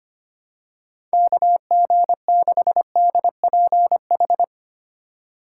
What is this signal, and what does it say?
Morse code, heard as a keyed tone, about 25 wpm, 700 Hz. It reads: KG6DPH
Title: USA Call Signs - 6 Characters - 3x Character Spacing - 25wpm